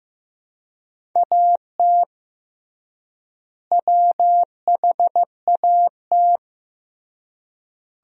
Morse code, heard as a keyed tone, about 15 wpm, 700 Hz. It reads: AT WHAT
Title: 3x Word-Spacing - Sentences from Top 100 Words - 15wpm